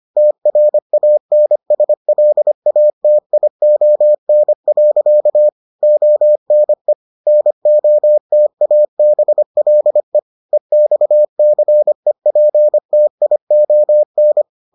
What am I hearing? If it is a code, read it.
Morse code, heard as a keyed tone, about 25 wpm, 600 Hz. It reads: TRANSLATION. ONE NOTABLE EXCEPTION